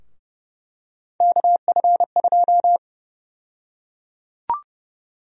Morse code, heard as a keyed tone, about 29 wpm, 700 Hz. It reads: KF2
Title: USA Call Sign Prefixes - 30wpm